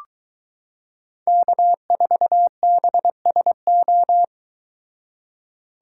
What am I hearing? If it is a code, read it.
Morse code, heard as a keyed tone, about 23 wpm, 700 Hz. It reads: K4BSO